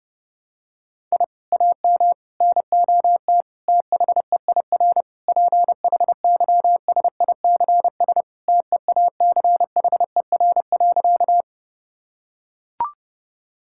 Morse code, heard as a keyed tone, about 30 wpm, 700 Hz. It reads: I AM NOT THEIR PHYSICS TEACHER.
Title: Sentences from Top 1000 Words - Review Part 6 - 30wpm